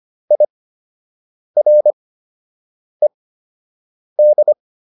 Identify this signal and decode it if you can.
Morse code, heard as a keyed tone, about 25 wpm, spaced as for 7 wpm, 600 Hz. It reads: IRED